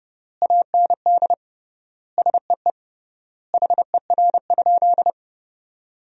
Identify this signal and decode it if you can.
Morse code, heard as a keyed tone, about 30 wpm, 700 Hz. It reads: AND SEE HER?
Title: 3x Word-Spacing - Sentences from Top 100 Words - 30wpm